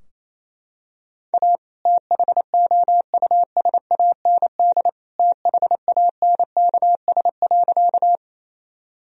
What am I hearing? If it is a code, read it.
Morse code, heard as a keyed tone, about 28 wpm, 700 Hz. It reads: A THOUSAND THANKS.
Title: Sentences from Top 1000 Words - Review Part 3 - 28wpm